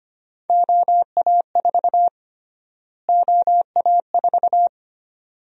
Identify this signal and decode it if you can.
Morse code, heard as a keyed tone, about 25 wpm, 700 Hz. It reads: OA4 OA4
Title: USA Call Sign Prefixes - Repeated 2x - 25wpm